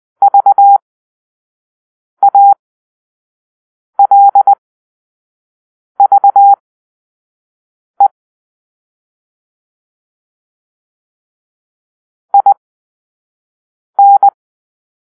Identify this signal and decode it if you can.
Morse code, heard as a keyed tone, about 20 wpm, spaced as for 5 wpm, 800 Hz. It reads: VALVE IN
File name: arrl / Grp-5 Drill-3